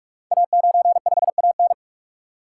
Morse code, heard as a keyed tone, about 44 wpm, 700 Hz. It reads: A95AN
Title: Code Groups - 5 Letter:Numbers - 45wpm